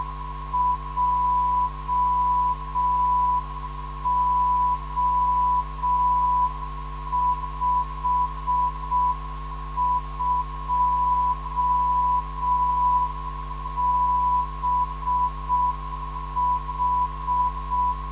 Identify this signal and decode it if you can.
Morse code, heard as a keyed tone, about 6 wpm, 990 Hz. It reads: JO52BH
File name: ndb zvok / DI2BO 440kHz